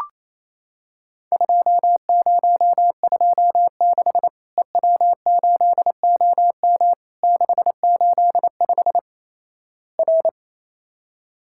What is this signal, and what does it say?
Morse code, heard as a keyed tone, about 27 wpm, 700 Hz. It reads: 2026 EW8OM 685 R
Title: Worked All Europe DX Contest - QTC Exchange - 28wpm